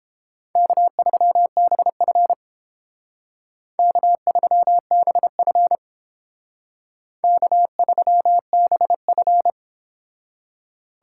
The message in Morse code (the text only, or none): K3BF K3BF K3BF